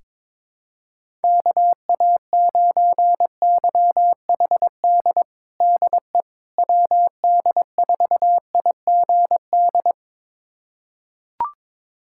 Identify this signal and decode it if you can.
Morse code, heard as a keyed tone, about 22 wpm, 700 Hz. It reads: KA9YHD DE WD4IGD
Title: USA Callsign DE Callsign 22wpm